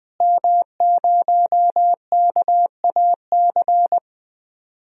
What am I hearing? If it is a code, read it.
Morse code, heard as a keyed tone, about 20 wpm, 700 Hz. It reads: M0KAC
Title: DXCC-100 Call Signs - 5 Characters - Repeated 3x - 20wpm